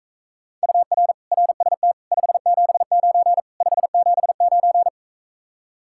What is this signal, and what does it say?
Morse code, heard as a keyed tone, about 45 wpm, 700 Hz. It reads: UR RST 579 579